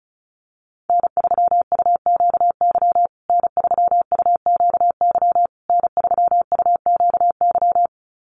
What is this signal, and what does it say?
Morse code, heard as a keyed tone, about 34 wpm, 700 Hz. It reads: N3UQY N3UQY N3UQY
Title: USA Call Signs Repeated 3x 35wpm